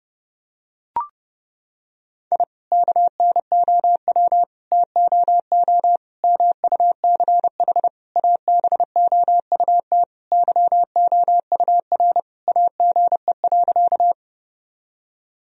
Morse code, heard as a keyed tone, about 29 wpm, 700 Hz. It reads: I KNOW TOO MUCH ABOUT YOUR AGE.
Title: Sentences from Top 1000 Words - Review Part 3 - 30wpm